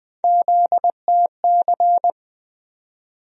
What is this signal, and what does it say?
Morse code, heard as a keyed tone, about 20 wpm, 700 Hz. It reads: ZTC
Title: USA Call Sign Postfixes - 3 Letters 20wpm